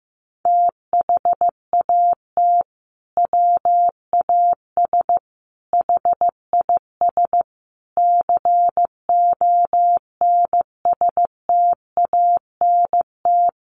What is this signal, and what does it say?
Morse code, heard as a keyed tone, about 14 wpm, 700 Hz. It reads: THAT WAS HIS CONSTANT
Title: Sentences from Top 1500 Words 15wpm